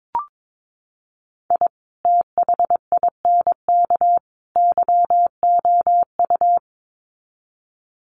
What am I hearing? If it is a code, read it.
Morse code, heard as a keyed tone, about 22 wpm, 700 Hz. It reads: I THINK YOU
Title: Sets of 3 Words 22wpm